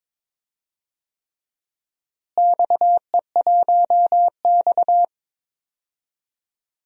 Morse code, heard as a keyed tone, about 22 wpm, 700 Hz. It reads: XE1X
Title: DXCC-100 Call Signs - 4 Characters - Repeated 3x - 22wpm